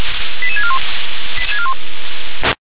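Five-tone selective call (ZVEI), heard as quick runs of five tones, tones of about 70 ms. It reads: 88641
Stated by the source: ZVEI five-tone selective call received by SDR.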